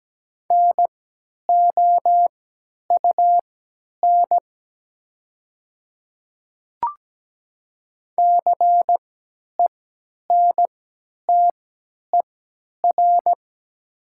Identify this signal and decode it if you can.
Morse code, heard as a keyed tone, about 17 wpm, spaced as for 10 wpm, 700 Hz. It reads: NOUN CENTER
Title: 3x Character Spacing - Top 500 Words - 17wpm